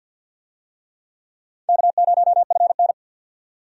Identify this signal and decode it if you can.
Morse code, heard as a keyed tone, about 47 wpm, 700 Hz. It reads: K0FN
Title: USA Call Signs - 4 Characters - 50wpm